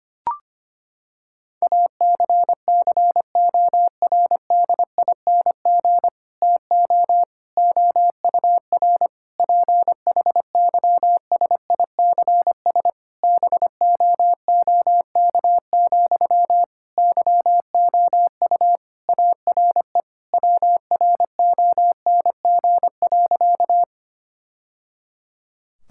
Morse code, heard as a keyed tone, about 25 wpm, 700 Hz. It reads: ACCORDING TO OUR PHYSICS BOOK, YOU ARE WRONG.
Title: Sentences from Top 1000 Words - Review Part 5 - 25wpm